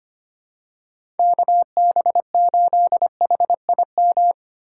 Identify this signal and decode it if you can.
Morse code, heard as a keyed tone, about 25 wpm, 700 Hz. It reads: KB8HIM
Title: USA Call Signs - 6 Characters - 4x Character Spacing - 25wpm